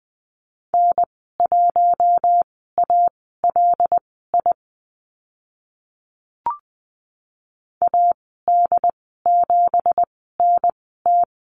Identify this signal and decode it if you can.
Morse code, heard as a keyed tone, about 20 wpm, spaced as for 15 wpm, 700 Hz. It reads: N1ALI AD7NT
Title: 186 Lesson 30g - Letter b - Call Signs 20wpm